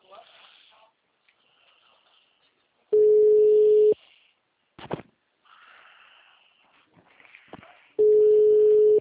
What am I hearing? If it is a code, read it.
Ringback tone (Europe, 425 Hz), with speech.